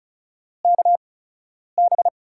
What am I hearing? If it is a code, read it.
Morse code, heard as a keyed tone, about 36 wpm, 700 Hz. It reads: K B
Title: USA Call Signs - 6 Characters - 8x Character Spacing - 35wpm